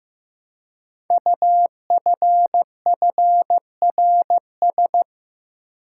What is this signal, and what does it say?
Morse code, heard as a keyed tone, about 15 wpm, 700 Hz. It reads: UFFRS
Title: Code Groups - 5 Letters - 15wpm